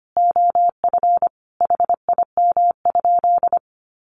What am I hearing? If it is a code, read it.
Morse code, heard as a keyed tone, about 25 wpm, 700 Hz. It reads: OF HIM?
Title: Sentences from Top 100 Words 25wpm